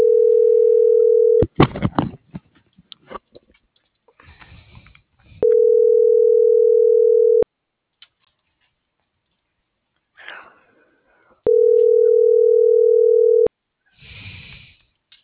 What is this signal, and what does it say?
Ringback tone (North America), 440 + 480 Hz.